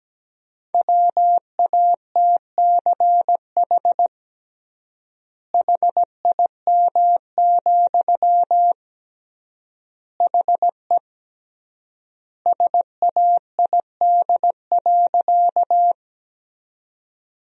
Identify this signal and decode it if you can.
Morse code, heard as a keyed tone, about 17 wpm, 700 Hz. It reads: WATCH HIM, HE SAID.
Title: 3x Word-Spacing - Sentences from Top 300 Words - 17wpm